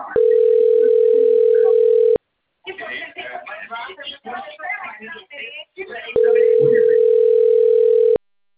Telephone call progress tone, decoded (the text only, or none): ringback tone